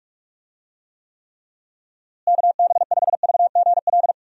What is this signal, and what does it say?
Morse code, heard as a keyed tone, about 46 wpm, 700 Hz. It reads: KB5VGL
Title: USA Call Signs - 6 Characters - Repeated 3x - 45wpm